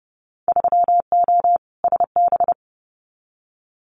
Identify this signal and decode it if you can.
Morse code, heard as a keyed tone, about 32 wpm, 700 Hz. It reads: 3O SB
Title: Winter Field Day Exchanges Encore 30wpm